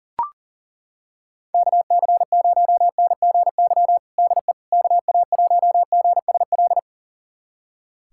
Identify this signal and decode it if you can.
Morse code, heard as a keyed tone, about 40 wpm, 700 Hz. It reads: KC0NGY DE KA1GSL
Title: USA Callsign DE Callsign - 6 Characters - 40wpm